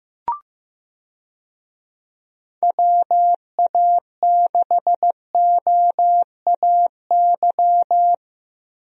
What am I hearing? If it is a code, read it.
Morse code, heard as a keyed tone, about 15 wpm, 700 Hz. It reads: WA6OAY
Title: USA Call Signs - 6 Characters - Repeated 2x - 15wpm